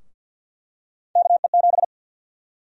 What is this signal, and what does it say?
Morse code, heard as a keyed tone, about 46 wpm, 700 Hz. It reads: KE7
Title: USA Call Sign Prefixes - 50wpm